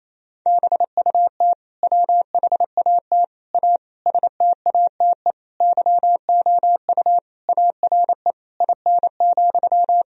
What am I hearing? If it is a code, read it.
Morse code, heard as a keyed tone, about 27 wpm, 700 Hz. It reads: BUT WHAT A STATE YOU ARE IN,
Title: Sentences from Top 1000 Words - Review Part 2 - 28wpm